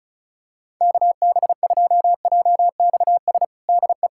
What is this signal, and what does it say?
Morse code, heard as a keyed tone, about 34 wpm, 700 Hz. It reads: KB2JXS DE
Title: USA Callsign DE Callsign 35wpm